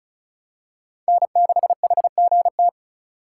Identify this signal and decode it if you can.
Morse code, heard as a keyed tone, about 36 wpm, 700 Hz. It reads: N6HGT
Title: USA Call Signs - 5 Characters - 2x Character Spacing - 35wpm